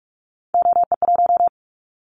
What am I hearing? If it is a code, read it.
Morse code, heard as a keyed tone, about 46 wpm, 700 Hz. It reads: OE1
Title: Callsign Prefixes for Top 50 DXCC - 45wpm